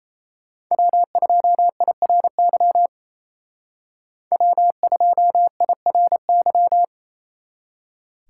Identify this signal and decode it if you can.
Morse code, heard as a keyed tone, about 33 wpm, 700 Hz. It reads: W2IRY W2IRY
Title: Speed Racing - USA 1x3 Call Signs 28wpm